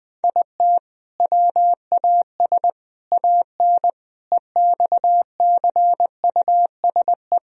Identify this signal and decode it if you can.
Morse code, heard as a keyed tone, about 21 wpm, 700 Hz. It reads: IT WAS AN EXCUSE
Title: Sentences from Top 1000 Words - Review Part 1 - 20wpm